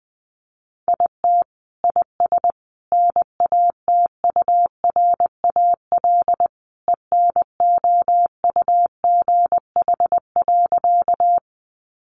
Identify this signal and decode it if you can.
Morse code, heard as a keyed tone, about 20 wpm, 700 Hz. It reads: IT IS NATURAL ENOUGH.
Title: Sentences from Top 700 Words 20wpm